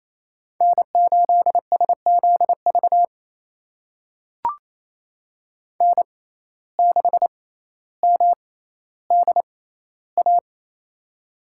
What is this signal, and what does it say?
Morse code, heard as a keyed tone, about 28 wpm, spaced as for 14 wpm, 700 Hz. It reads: N8SZV N6MDA
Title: USA Call Signs - 6x Character Spacing - 28wpm